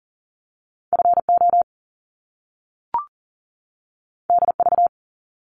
Morse code, heard as a keyed tone, about 40 wpm, 700 Hz. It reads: FO DV
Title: Top Two Letter Combinations in Words 40wpm